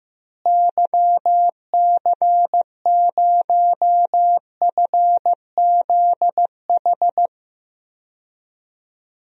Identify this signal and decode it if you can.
Morse code, heard as a keyed tone, about 15 wpm, 700 Hz. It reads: YC0FZH
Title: DXCC-100 Call Signs - 6 Characters - Repeated 2x - 15wpm